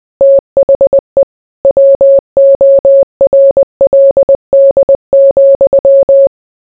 Morse code, heard as a keyed tone, about 20 wpm, 550 Hz. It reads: THE WORLD,